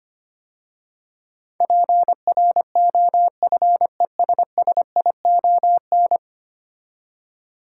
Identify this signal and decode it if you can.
Morse code, heard as a keyed tone, about 25 wpm, 700 Hz. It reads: PROFESSION
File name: 10 Letter Words - First 3 Letters Given - 25wpm